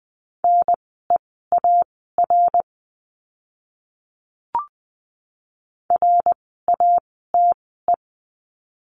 Morse code, heard as a keyed tone, about 20 wpm, spaced as for 14 wpm, 700 Hz. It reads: NEAR RATE
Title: 054 Lesson 10f - Letter r - Words 20wpm